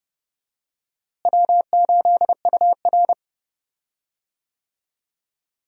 Morse code, heard as a keyed tone, about 32 wpm, 700 Hz. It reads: W8UR
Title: USA Call Signs - 4 Characters - Repeated 2x - 30wpm